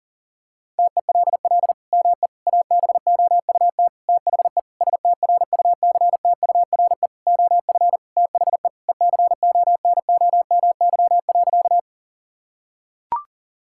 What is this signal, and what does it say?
Morse code, heard as a keyed tone, about 37 wpm, 700 Hz. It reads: TELL ME ABOUT THE STRUCTURE OF THE ECONOMY.
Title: Sentences from Top 1000 Words - Review Part 6 - 40wpm